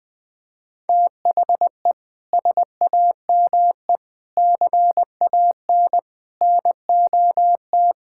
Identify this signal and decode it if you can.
Morse code, heard as a keyed tone, about 20 wpm, 700 Hz. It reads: THE SAME CAN NOT
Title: Sets of 8 Words - 20wpm